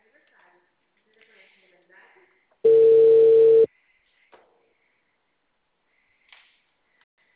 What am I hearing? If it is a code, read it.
Ringback tone (China, 450 Hz), with speech.